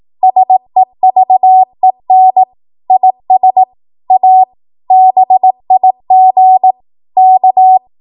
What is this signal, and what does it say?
Morse code, heard as a keyed tone, about 18 wpm, 750 Hz. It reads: SEVEN IS A BIG K
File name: Archive / 18 WPM / 18WPM